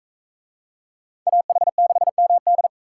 Morse code, heard as a keyed tone, about 41 wpm, 700 Hz. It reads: AH6MD